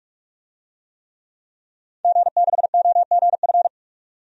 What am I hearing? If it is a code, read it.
Morse code, heard as a keyed tone, about 45 wpm, 700 Hz. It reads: G6OGF
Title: DXCC-100 Call Signs - 5 Characters - 45wpm